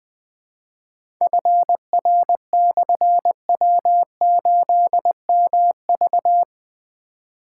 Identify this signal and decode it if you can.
Morse code, heard as a keyed tone, about 20 wpm, 700 Hz. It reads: FR/W8MV
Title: Difficult Call Signs 20wpm